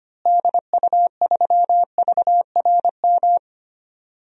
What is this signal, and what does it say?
Morse code, heard as a keyed tone, about 25 wpm, 700 Hz. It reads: DU3VRM
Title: DXCC-100 Call Signs - 6 Characters - Repeated 3x - 25wpm